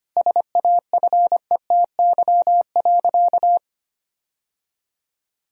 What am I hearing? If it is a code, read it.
Morse code, heard as a keyed tone, about 25 wpm, 700 Hz. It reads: SAFETY.
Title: Sentences from Top 1000 Words - Review Part 4 - 25wpm